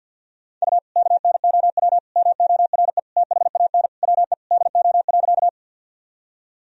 Morse code, heard as a keyed tone, about 53 wpm, 700 Hz. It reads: U KNOW MORE THAN WE DO.
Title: Sentences from Top 100 Words and Abbreviations 50wpm